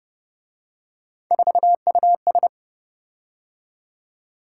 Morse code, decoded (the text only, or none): 4US